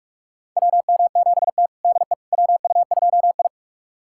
Morse code, heard as a keyed tone, about 46 wpm, 700 Hz. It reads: WM7T DE WU2I
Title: USA Callsign DE Callsign - 4 Characters - 45wpm